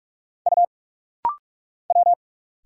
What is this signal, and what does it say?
Morse code, heard as a keyed tone, about 44 wpm, 700 Hz. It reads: U W
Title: Single Letter-Number Alt - Rapid-Fire 45wpm